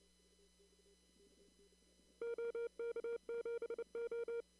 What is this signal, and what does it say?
Morse code, heard as a keyed tone, about 29 wpm, 470 Hz. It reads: OK7O
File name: CW / kt5j wpx cw-op-k5pi